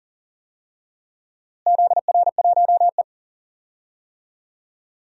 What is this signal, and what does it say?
Morse code, heard as a keyed tone, about 40 wpm, 700 Hz. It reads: ZR1E
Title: DXCC-100 Call Signs - 4 Characters - Repeated 2x - 40wpm